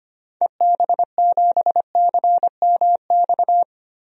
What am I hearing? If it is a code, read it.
Morse code, heard as a keyed tone, about 25 wpm, 700 Hz. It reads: EB7CMX
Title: DXCC-100 Call Signs - 6 Characters - 25wpm